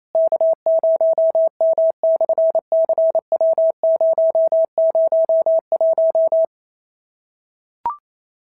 Morse code, heard as a keyed tone, about 27 wpm, 650 Hz. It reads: K0M/CW001
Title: SOTA QSO Practice 28wpm